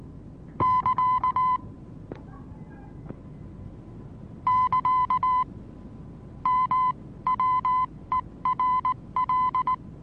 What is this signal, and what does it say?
Morse code, heard as a keyed tone, about 19 wpm, spaced as for 16 wpm, 1010 Hz. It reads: <KA> <KA> MWERL
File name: repos / Morsekurs / DJ4UF Morsen26